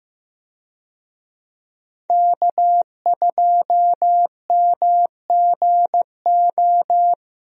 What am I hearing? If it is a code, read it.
Morse code, heard as a keyed tone, about 16 wpm, 700 Hz. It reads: K2MGO
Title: USA Call Signs - 5 Characters - 15wpm